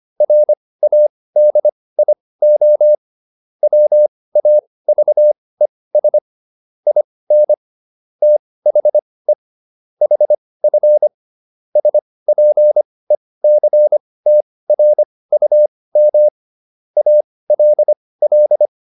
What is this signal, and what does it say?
Morse code, heard as a keyed tone, about 25 wpm, spaced as for 18 wpm, 600 Hz. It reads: RADIO WAVES IN THE HF SPECTRUM ALL